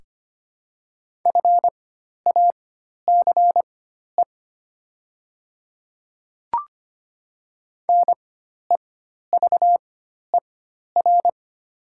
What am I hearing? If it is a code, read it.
Morse code, heard as a keyed tone, about 25 wpm, spaced as for 12 wpm, 700 Hz. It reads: FACE NEVER
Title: 4x Character Spacing - Top 300 Words - 25wpm